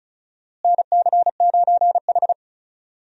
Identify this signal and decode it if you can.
Morse code, heard as a keyed tone, about 36 wpm, 700 Hz. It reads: NC9H
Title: USA Call Signs - 4 Characters - Repeated 3x - 35wpm